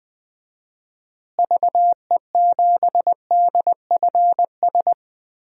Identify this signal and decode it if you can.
Morse code, heard as a keyed tone, about 20 wpm, 700 Hz. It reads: VE7DFS
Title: DXCC-100 Call Signs - 6 Characters - Repeated 3x - 20wpm